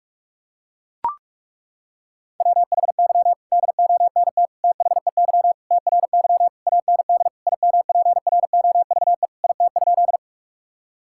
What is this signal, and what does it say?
Morse code, heard as a keyed tone, about 41 wpm, 700 Hz. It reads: WHY DONT THEY TRY AND IMPROVE IT?
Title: Sentences from Top 1000 Words - Review Part 5 - 45wpm